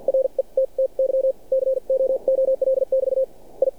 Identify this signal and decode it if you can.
Morse code, heard as a keyed tone, about 47 wpm, spaced as for 39 wpm, 530 Hz. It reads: RETTY COOL<BT> I
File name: archived / CRSnet